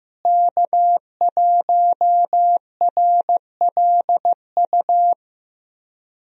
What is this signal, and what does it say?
Morse code, heard as a keyed tone, about 15 wpm, 700 Hz. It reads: K1RLU
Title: USA Call Signs - 5 Characters - Repeated 2x - 15wpm